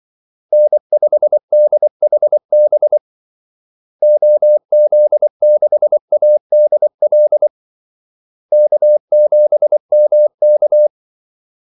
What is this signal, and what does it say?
Morse code, heard as a keyed tone, about 24 wpm, 600 Hz. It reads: N5DHB OZ6ADL K7MK
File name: Callsigns-24wpm